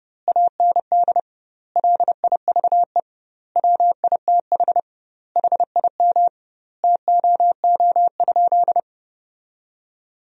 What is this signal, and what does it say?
Morse code, heard as a keyed tone, about 32 wpm, 700 Hz. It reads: AND LIVE WITH HIM TOO?
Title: 2x Word-Spacing - Sentences from Top 500 Words - 30wpm